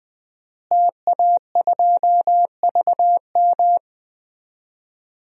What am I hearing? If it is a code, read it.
Morse code, heard as a keyed tone, about 20 wpm, 700 Hz. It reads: TA2VM
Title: DXCC-100 Call Signs - 5 Characters - Repeated 3x - 20wpm